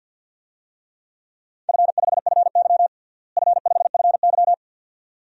Morse code, heard as a keyed tone, about 47 wpm, 700 Hz. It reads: F5FY F5FY
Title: DXCC-100 Call Signs - 4 Characters - Repeated 2x - 50wpm